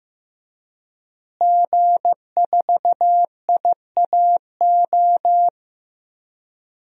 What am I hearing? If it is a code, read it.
Morse code, heard as a keyed tone, about 15 wpm, 700 Hz. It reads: G4IAO
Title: DXCC-100 Call Signs - 5 Characters - Repeated 2x - 15wpm